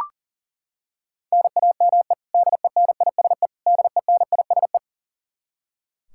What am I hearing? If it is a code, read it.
Morse code, heard as a keyed tone, about 40 wpm, 700 Hz. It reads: NAME DENISE DENISE
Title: Practice QSO - Exchange 1 - Beginning - 40wpm